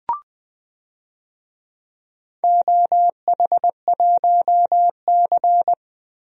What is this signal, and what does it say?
Morse code, heard as a keyed tone, about 20 wpm, 700 Hz. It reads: OH1C
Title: DXCC-100 Call Signs - 4 Characters - Repeated 3x - 20wpm